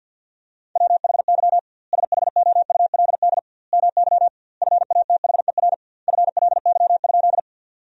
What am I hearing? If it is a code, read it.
Morse code, heard as a keyed tone, about 47 wpm, 700 Hz. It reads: WHY SHOULD MY FATHER FLY?
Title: 2x Word-Spacing - Sentences from Top 500 Words - 50wpm